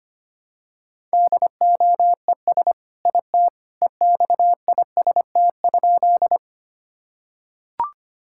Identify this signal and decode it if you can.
Morse code, heard as a keyed tone, about 25 wpm, 700 Hz. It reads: DOES IT EXIST?
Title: Sentences from Top 1000 Words - Review Part 5 - 25wpm